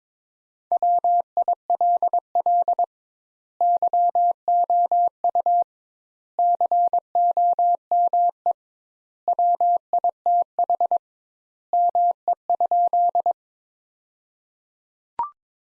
Morse code, heard as a keyed tone, about 22 wpm, 700 Hz. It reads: WILL YOU COME WITH ME?
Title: Sentences Top 200 Words - 22wpm